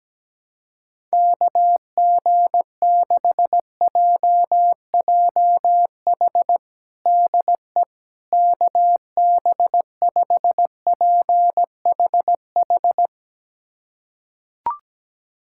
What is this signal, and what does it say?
Morse code, heard as a keyed tone, about 17 wpm, 700 Hz. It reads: KG6JJH DE KB5PHH
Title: USA Callsign DE Callsign 17wpm